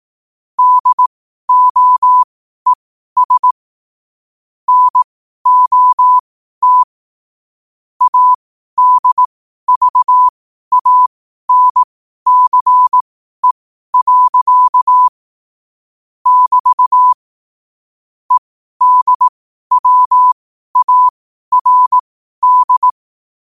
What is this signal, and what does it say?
Morse code, heard as a keyed tone, about 18 wpm, spaced as for 12 wpm, 1000 Hz. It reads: DOES NOT ADVANCE. <BT> EDWARD
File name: qotdfiles / 12wpm